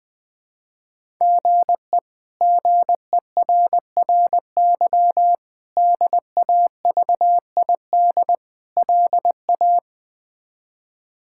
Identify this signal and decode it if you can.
Morse code, heard as a keyed tone, about 20 wpm, 700 Hz. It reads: GE GERRY DAVID LA
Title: K1USN SST QSO Practice - 20wpm